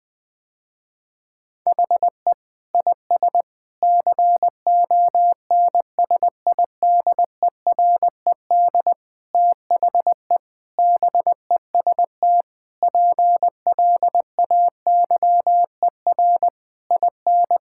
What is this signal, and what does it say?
Morse code, heard as a keyed tone, about 19 wpm, 700 Hz. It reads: HE IS CONSIDERED THE BEST PLAYER IN
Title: Sentences with 6-Word Phrases - Spoken Beginning - 20wpm